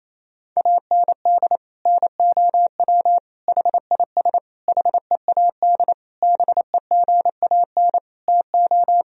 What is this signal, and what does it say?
Morse code, heard as a keyed tone, about 28 wpm, 700 Hz. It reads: AND NOW HIS HEAD BEGAN TO
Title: Sentences from Top 1000 Words - Review Part 1 - 28wpm